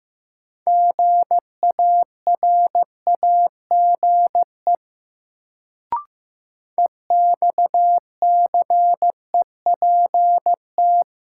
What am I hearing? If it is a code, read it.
Morse code, heard as a keyed tone, about 15 wpm, 700 Hz. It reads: GARAGE EXCEPT
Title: Sentences from Top 1600 Words - New Word Drill - 15wpm